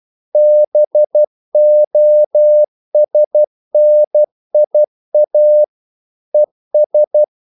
Morse code, heard as a keyed tone, about 12 wpm, 600 Hz. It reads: BOSNIA ES
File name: CW / Practice / NationalCapitals12